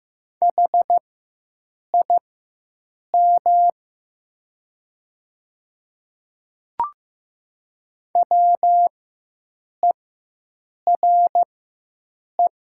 Morse code, heard as a keyed tone, about 15 wpm, spaced as for 7 wpm, 700 Hz. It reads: HIM WERE
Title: 4x Character Spacing - Top 200 Words - 15wpm